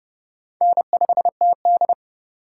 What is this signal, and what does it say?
Morse code, heard as a keyed tone, about 29 wpm, 700 Hz. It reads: N5TD